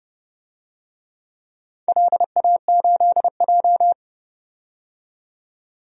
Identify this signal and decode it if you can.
Morse code, heard as a keyed tone, about 32 wpm, 700 Hz. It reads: LA8J
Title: DXCC-100 Call Signs - 4 Characters - 30wpm